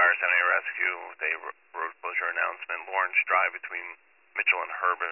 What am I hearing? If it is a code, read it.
Emergency Alert System (SAME) tones, heard as harsh buzzing bursts, with speech. It is heard from 0.6 to 3.9 s.